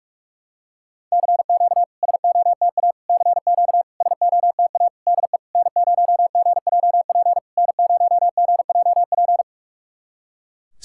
Morse code, heard as a keyed tone, about 44 wpm, 690 Hz. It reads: CQ SOTA CQ SOTA DE N0GJP N0GJP
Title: SOTA QSO Practice 45wpm